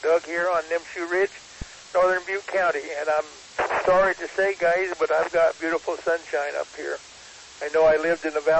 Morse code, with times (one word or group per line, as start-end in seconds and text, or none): none